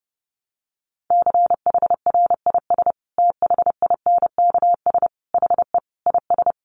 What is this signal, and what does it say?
Morse code, heard as a keyed tone, about 30 wpm, 700 Hz. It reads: CHRIS THINKS HE IS
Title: Easy Phrases 30wpm